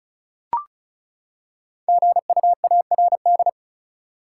Morse code, heard as a keyed tone, about 36 wpm, 700 Hz. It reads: GUARD
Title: Practice QSO - Exchange 3 - More Info - 35wpm